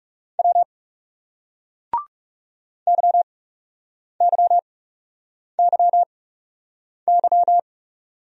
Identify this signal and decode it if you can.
Morse code, heard as a keyed tone, about 40 wpm, spaced as for 8 wpm, 700 Hz. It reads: W YYYY